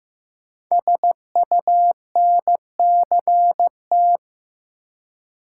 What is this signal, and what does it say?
Morse code, heard as a keyed tone, about 15 wpm, 700 Hz. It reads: SUNCT
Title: Code Groups - 5 Letter:Numbers - 15wpm